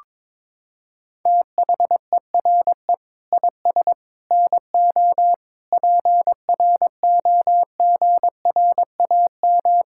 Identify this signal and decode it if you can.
Morse code, heard as a keyed tone, about 22 wpm, 700 Hz. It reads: THERE IS NO PROGRAM
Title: Sentences from Top 1000 Words - Review Part 5 - 22wpm